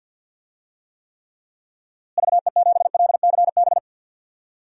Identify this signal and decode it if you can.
Morse code, heard as a keyed tone, about 50 wpm, 700 Hz. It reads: VE7LCB
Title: DXCC-100 Call Signs - 6 Characters - Repeated 3x - 50wpm